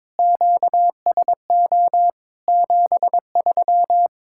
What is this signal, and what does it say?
Morse code, heard as a keyed tone, about 22 wpm, 700 Hz. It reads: QSO 73
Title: Practice QSO - Exchange 4 - Ending - 22wpm